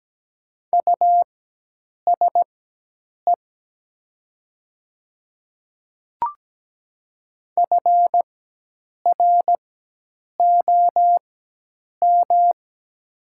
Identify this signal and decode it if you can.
Morse code, heard as a keyed tone, about 17 wpm, spaced as for 8 wpm, 700 Hz. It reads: USE FROM
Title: 4x Character Spacing - Top 200 Words - 17wpm